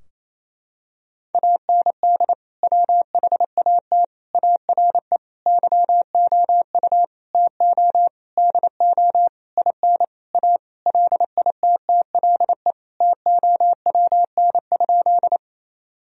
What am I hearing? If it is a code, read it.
Morse code, heard as a keyed tone, about 28 wpm, 700 Hz. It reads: AND WHAT ARE YOU TO DO IN A LITTLE TOWN?
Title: Sentences from Top 1000 Words - Review Part 2 - 28wpm